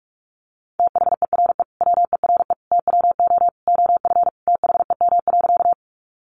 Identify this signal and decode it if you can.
Morse code, heard as a keyed tone, about 44 wpm, 700 Hz. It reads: THERE WERE TWO OF THEM.
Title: Sentences from Top 100 Words 45wpm